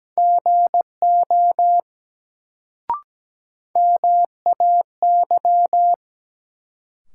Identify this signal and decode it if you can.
Morse code, heard as a keyed tone, about 17 wpm, 700 Hz. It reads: GO MAY
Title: Sets of 2 Words - New Word Drill - 17wpm